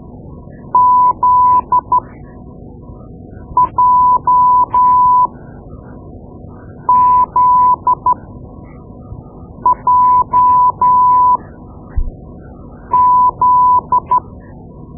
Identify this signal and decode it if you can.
Morse code, heard as a keyed tone, about 13 wpm, spaced as for 5 wpm, 1000 Hz. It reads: ZJZJZ
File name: Morse Lessons / Morse Lessons / Tape-2B